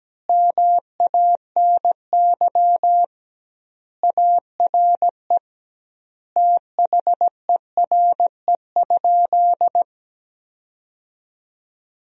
Morse code, heard as a keyed tone, about 17 wpm, 700 Hz. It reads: MANY ARE THERE?
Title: 2x Word-Spacing - Sentences from Top 100 Words - 17wpm